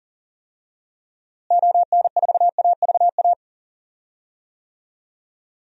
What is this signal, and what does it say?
Morse code, heard as a keyed tone, about 40 wpm, 700 Hz. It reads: ON4AVA
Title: DXCC-100 Call Signs - 6 Characters - Repeated 3x - 40wpm